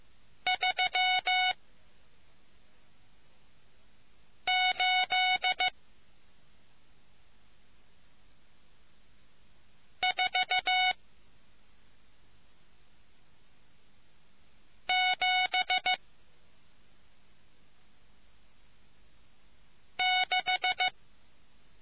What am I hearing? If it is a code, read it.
Morse code, heard as a keyed tone, about 15 wpm, 740 Hz. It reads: 3 8 4 7 6